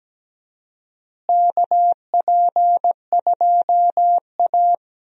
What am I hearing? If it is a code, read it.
Morse code, heard as a keyed tone, about 17 wpm, 700 Hz. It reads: KP2A
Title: DXCC-100 Call Signs - 4 Characters - Repeated 3x - 17wpm